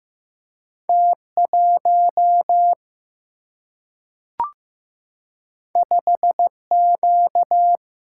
Morse code, heard as a keyed tone, about 15 wpm, 700 Hz. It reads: T 1 5 Q
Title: Single-Letter Single-Number 15wpm